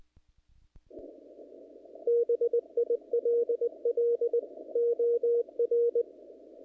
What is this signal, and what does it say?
Morse code, heard as a keyed tone, about 19 wpm, 470 Hz. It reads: BILL OR
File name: contests / naqp-cw-k5tr-op-k5pi- -1900